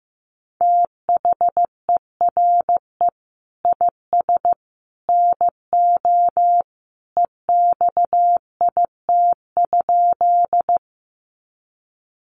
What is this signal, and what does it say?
Morse code, heard as a keyed tone, about 15 wpm, 700 Hz. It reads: THERE IS NO EXIT?
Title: Sentences from Top 1100 Words 15wpm